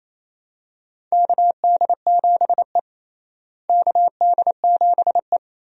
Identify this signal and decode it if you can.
Morse code, heard as a keyed tone, about 28 wpm, 700 Hz. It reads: KD7E KD7E
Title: USA Call Signs - 4 Characters - Repeated 2x - 28wpm